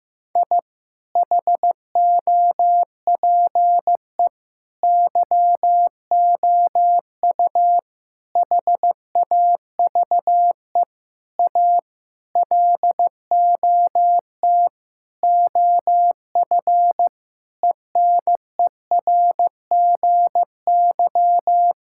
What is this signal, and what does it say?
Morse code, heard as a keyed tone, about 15 wpm, 700 Hz. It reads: I HOPE YOU HAVE A LOT OF ENERGY
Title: Sentences from Top 1000 Words - Review Part 6 - 15wpm